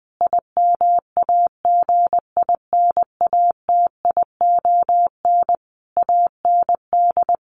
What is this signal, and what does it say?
Morse code, heard as a keyed tone, about 20 wpm, 700 Hz. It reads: IMAGINATION AND
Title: Sentences from Top 1300 Words 20wpm